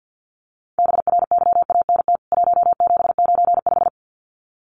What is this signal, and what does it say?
Morse code, heard as a keyed tone, about 50 wpm, 700 Hz. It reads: BRYANT 1795
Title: CWT Exchange 50wpm